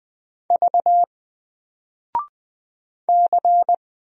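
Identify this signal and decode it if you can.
Morse code, heard as a keyed tone, about 20 wpm, 700 Hz. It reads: V C
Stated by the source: Single letter and number Morse drill at 20 wpm, from N4LCD.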